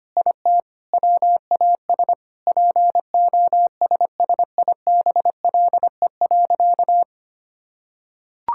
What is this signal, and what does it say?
Morse code, heard as a keyed tone, about 25 wpm, 700 Hz. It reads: IT WAS POSSIBLE.
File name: practice / 25wpm / Sentences with 4-Word Phrases - 25wpm